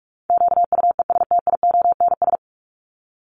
Morse code, heard as a keyed tone, about 44 wpm, 700 Hz. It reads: QUESTIONS
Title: ES:S Suffix - Top 100 Words with Base Words 45wpm